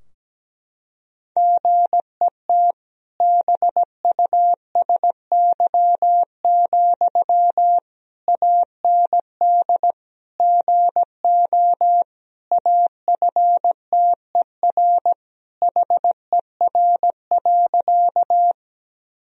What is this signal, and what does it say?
Morse code, heard as a keyed tone, about 17 wpm, 700 Hz. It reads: GET BUSY, AND GO AFTER HER.
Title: Sentences from Top 1000 Words - Review Part 3 - 17wpm